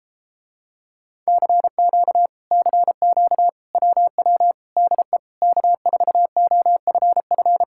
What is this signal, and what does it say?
Morse code, heard as a keyed tone, about 33 wpm, 700 Hz. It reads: CQ CQ WW DE K4OFF